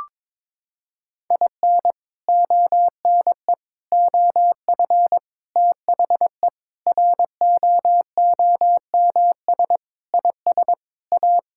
Morse code, heard as a keyed tone, about 22 wpm, 700 Hz. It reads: IN ONE OF THE ROOMS IS A
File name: practice / 22wpm / Sentences with 4-Word Phrases - 22wpm